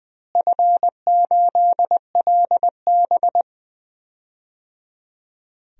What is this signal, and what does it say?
Morse code, heard as a keyed tone, about 21 wpm, 700 Hz. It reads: F8LB
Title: DXCC-100 Call Signs - 4 Characters - Repeated 3x - 20wpm